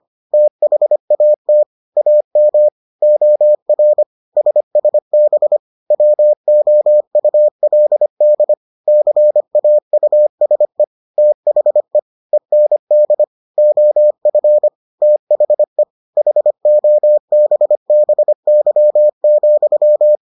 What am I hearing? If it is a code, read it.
Morse code, heard as a keyed tone, about 25 wpm, 600 Hz. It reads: THAT AM OR SSB WOULD CAUSE THE END OF THE HOBBY,